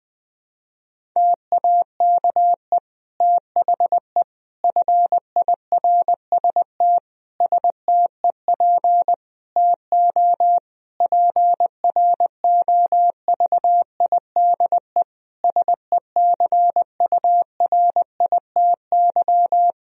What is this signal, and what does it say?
Morse code, heard as a keyed tone, about 20 wpm, 700 Hz. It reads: TAKE THE FIRST STEP TO PROVIDE SECURITY
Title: Sentences from Top 1000 Words - Review Part 4 - 20wpm